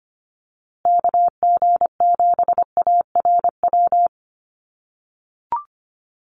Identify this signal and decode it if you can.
Morse code, heard as a keyed tone, about 25 wpm, 700 Hz. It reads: KG7ARW
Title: USA Call Signs Repeated 3x 25wpm